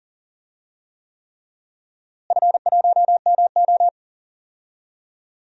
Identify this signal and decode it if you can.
Morse code, heard as a keyed tone, about 40 wpm, 700 Hz. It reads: F1MO
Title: DXCC-100 Call Signs - 4 Characters - Repeated 3x - 40wpm